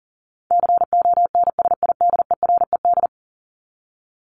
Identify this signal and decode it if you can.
Morse code, heard as a keyed tone, about 40 wpm, 700 Hz. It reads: CONSIDERED
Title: ED Suffix - Top 100 Words with Base Words 40wpm